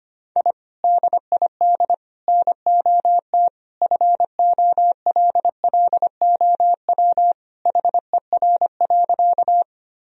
Morse code, heard as a keyed tone, about 25 wpm, 700 Hz. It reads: I DID NOT FOLLOW HER.
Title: Sentences from Top 1000 Words - Review Part 1 - 25wpm